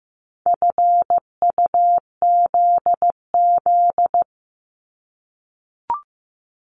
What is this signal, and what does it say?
Morse code, heard as a keyed tone, about 15 wpm, 700 Hz. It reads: FUZZ
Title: Multi-letter Phonemes Repeated 3x 15wpm